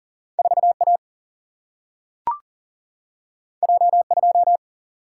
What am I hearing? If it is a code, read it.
Morse code, heard as a keyed tone, about 40 wpm, 700 Hz. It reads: 4 A J 2
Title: Single-Letter Single-Number 40wpm